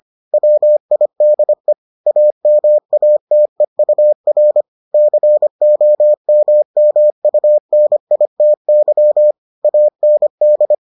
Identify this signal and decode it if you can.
Morse code, heard as a keyed tone, about 25 wpm, 600 Hz. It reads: WIDE AMATEUR COMMUNITY AND